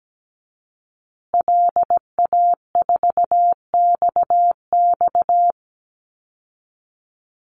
Morse code, heard as a keyed tone, about 17 wpm, 700 Hz. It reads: LA4XX
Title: Call Signs Repeated 2x - 17wpm